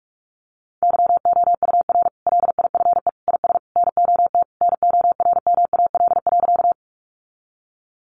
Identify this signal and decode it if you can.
Morse code, heard as a keyed tone, about 44 wpm, 700 Hz. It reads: YOUR LIFE IS NOT NORMAL.
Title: Sentences from Top 1200 Words 45wpm